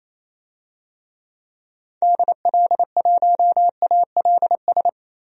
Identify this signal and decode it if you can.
Morse code, heard as a keyed tone, about 28 wpm, 700 Hz. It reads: DL1ALS
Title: DXCC-100 Call Signs - 6 Characters - Repeated 3x - 28wpm